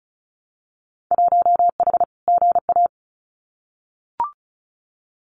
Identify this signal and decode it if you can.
Morse code, heard as a keyed tone, about 36 wpm, 690 Hz. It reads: 1H GA
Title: Winter Field Day Exchanges 35wpm